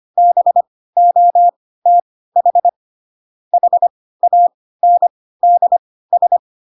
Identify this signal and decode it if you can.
Morse code, heard as a keyed tone, about 25 wpm, spaced as for 16 wpm, 700 Hz. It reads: BOTH HANDS
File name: Pinocchio / Pinocchio ch1 10 25 ik1hge